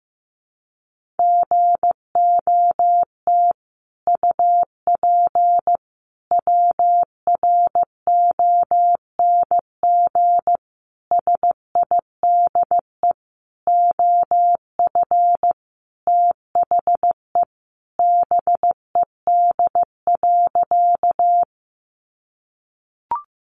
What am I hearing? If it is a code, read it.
Morse code, heard as a keyed tone, about 15 wpm, 700 Hz. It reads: GOT UP WRONG SIDE OF THE BED.
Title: Sentences from Top 900 Words 15wpm